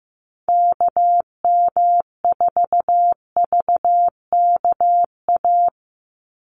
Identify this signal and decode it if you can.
Morse code, heard as a keyed tone, about 15 wpm, 700 Hz. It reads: KM4VKA
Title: USA Call Signs 15wpm